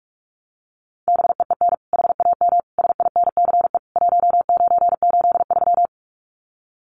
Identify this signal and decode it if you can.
Morse code, heard as a keyed tone, about 44 wpm, 700 Hz. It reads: BEEN HAM SINCE 1983
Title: Common QSO Phrases 45wpm